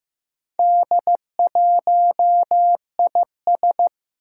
Morse code, heard as a keyed tone, about 15 wpm, 700 Hz. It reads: D1IS
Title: Four Challenging Letter-Number Combinations 15wpm